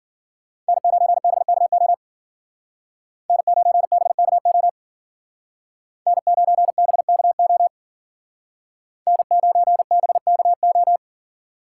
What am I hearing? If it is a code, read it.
Morse code, heard as a keyed tone, about 55 wpm, 700 Hz. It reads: N9BKO N9BKO N9BKO N9BKO